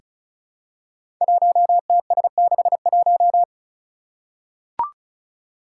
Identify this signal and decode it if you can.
Morse code, heard as a keyed tone, about 36 wpm, 700 Hz. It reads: 1TS61
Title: Five Challenging Letter-Number Combinations 35wpm